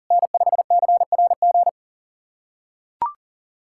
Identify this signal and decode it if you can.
Morse code, heard as a keyed tone, about 37 wpm, 700 Hz. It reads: N5CRG
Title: USA Callsign DE Callsign - 5 Characters - 40wpm